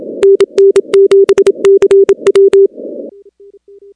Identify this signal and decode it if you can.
Morse code, heard as a keyed tone, about 27 wpm, 410 Hz. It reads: NN7CW
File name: contests / 2019-cw-wpx-kt5j-k5pi- -2330 0-15